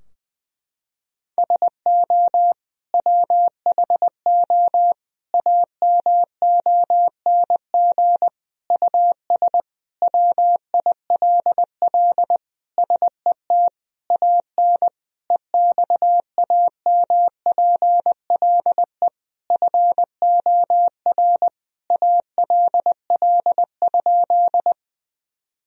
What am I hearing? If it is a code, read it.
Morse code, heard as a keyed tone, about 20 wpm, 700 Hz. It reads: SO WHO AMONG US WILL SET AN EXAMPLE FOR ALL?
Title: Sentences from Top 1000 Words - Review Part 3 - 20wpm